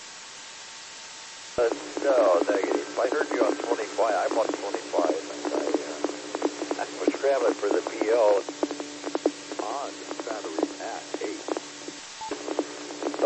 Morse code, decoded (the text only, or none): none